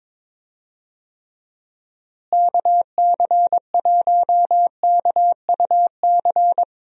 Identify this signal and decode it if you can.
Morse code, heard as a keyed tone, about 22 wpm, 700 Hz. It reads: KC1KUC
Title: USA Call Signs - 6 Characters - Repeated 3x - 22wpm